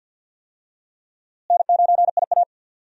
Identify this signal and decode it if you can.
Morse code, heard as a keyed tone, about 48 wpm, 700 Hz. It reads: N9IA
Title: USA Call Signs - 4 Characters - Repeated 2x - 50wpm